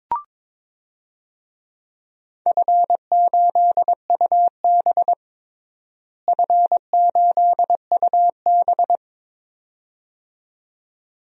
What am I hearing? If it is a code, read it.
Morse code, heard as a keyed tone, about 22 wpm, 700 Hz. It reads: F8UB F8UB
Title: DXCC-100 Call Signs - 4 Characters - Repeated 2x - 22wpm